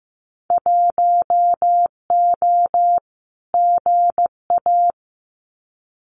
Morse code, heard as a keyed tone, about 15 wpm, 700 Hz. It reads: 1O GA
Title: Winter Field Day Exchanges 15wpm